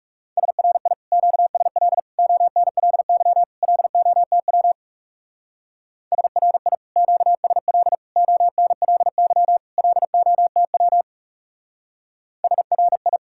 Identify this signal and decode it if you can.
Morse code, heard as a keyed tone, about 37 wpm, 700 Hz. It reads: SRI QSL ONLY LOTW SRI QSL ONLY LOTW SRI
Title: Speed-Racing Common QSO Phrases - 30wpm